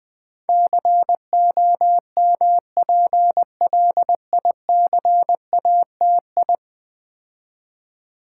Morse code, heard as a keyed tone, about 20 wpm, 700 Hz. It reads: COMPLICATI
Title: Build A Word - 5 Letters At A Time - 12-Letter Words - 20wpm